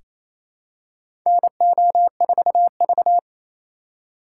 Morse code, heard as a keyed tone, about 28 wpm, 700 Hz. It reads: NO4V
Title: USA Call Signs - 2x Character Spacing - 28wpm